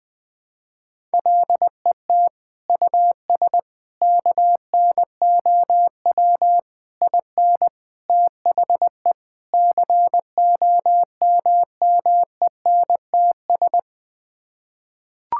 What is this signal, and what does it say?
Morse code, heard as a keyed tone, about 20 wpm, 700 Hz. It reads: LET US KNOW IN THE COMMENTS
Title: Sets of 6 Words - 20wpm